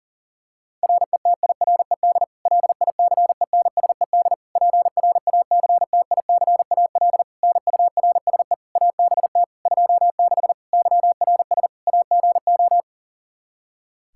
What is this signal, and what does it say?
Morse code, heard as a keyed tone, about 40 wpm, 700 Hz. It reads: RETIRED LICENSED PRACTICAL NURSE ABT 26 YRS AGO
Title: Practice QSO - Exchange 3 - More Info - 40wpm